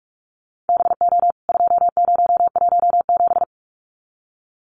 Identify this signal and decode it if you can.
Morse code, heard as a keyed tone, about 44 wpm, 700 Hz. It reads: BO 2017
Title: CWT Exchange 45wpm